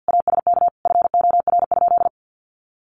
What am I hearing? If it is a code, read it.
Morse code, heard as a keyed tone, about 50 wpm, 700 Hz. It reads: ASK FOR?
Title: Sentences from Top 1400 Words 50wpm